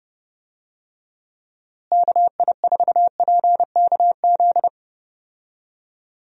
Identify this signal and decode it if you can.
Morse code, heard as a keyed tone, about 30 wpm, 700 Hz. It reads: KI4PKZ
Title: USA Call Signs - 6 Characters - Repeated 3x - 30wpm